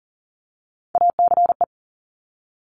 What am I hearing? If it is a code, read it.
Morse code, heard as a keyed tone, about 40 wpm, 700 Hz. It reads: ACE
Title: Top Three Letter Combinations in Words 40wpm